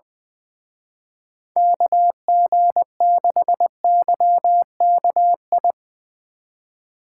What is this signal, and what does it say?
Morse code, heard as a keyed tone, about 22 wpm, 700 Hz. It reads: KG6YKI
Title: Speed Racing - USA 2x3 Call Signs 20wpm